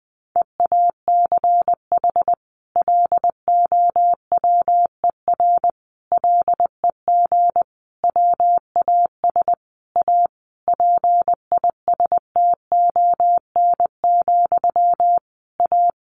Morse code, heard as a keyed tone, about 20 wpm, 700 Hz. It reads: EACH LOWER LEG WAS A PISTON, A
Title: A Sound Of Thunder 20wpm